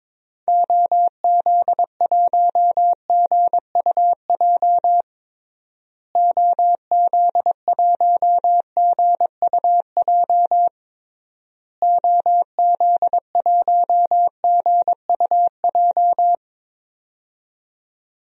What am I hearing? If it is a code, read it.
Morse code, heard as a keyed tone, about 22 wpm, 700 Hz. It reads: OZ1GUJ OZ1GUJ OZ1GUJ
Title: DXCC-100 Call Signs - 6 Characters - Repeated 3x - 22wpm